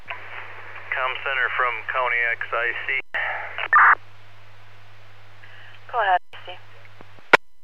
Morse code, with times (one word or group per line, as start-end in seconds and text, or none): none